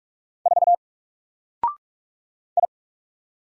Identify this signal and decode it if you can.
Morse code, heard as a keyed tone, about 46 wpm, 700 Hz. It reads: 4 I
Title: Single Character (Letter:Number:Punctuation:Prosign) 45wpm